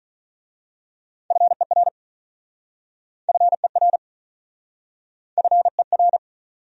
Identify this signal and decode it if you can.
Morse code, heard as a keyed tone, about 46 wpm, 700 Hz. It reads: FER FER FER